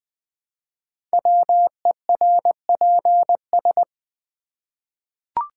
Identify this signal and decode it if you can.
Morse code, heard as a keyed tone, about 20 wpm, 700 Hz. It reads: WERPS
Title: Code Groups - 5 Letters - 20wpm